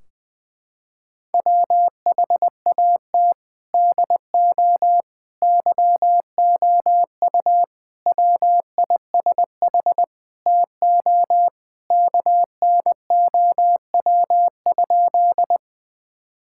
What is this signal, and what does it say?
Morse code, heard as a keyed tone, about 20 wpm, 700 Hz. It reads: WHAT DO YOU WISH TO KNOW?
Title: Sentences from Top 1600 Words 20wpm